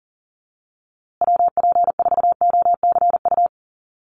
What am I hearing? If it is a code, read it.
Morse code, heard as a keyed tone, about 40 wpm, 700 Hz. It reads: WP4OCU
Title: USA Call Signs Repeated 3x 40wpm